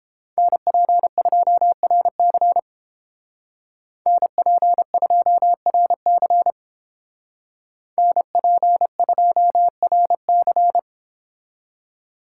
Morse code, heard as a keyed tone, about 30 wpm, 700 Hz. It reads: NP2RC NP2RC NP2RC